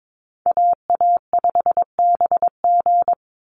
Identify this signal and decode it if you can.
Morse code, heard as a keyed tone, about 22 wpm, 700 Hz. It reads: AA5BG
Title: Call Signs - 22wpm